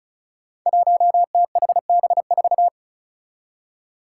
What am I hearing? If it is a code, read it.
Morse code, heard as a keyed tone, about 34 wpm, 700 Hz. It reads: 1THB4
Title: Five Challenging Letter-Number Combinations 35wpm